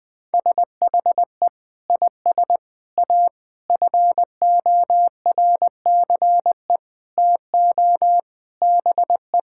Morse code, heard as a keyed tone, about 20 wpm, 700 Hz. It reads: SHE IS A FORCE TO BE
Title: Sentences with 6-Word Phrases - Spoken Beginning - 20wpm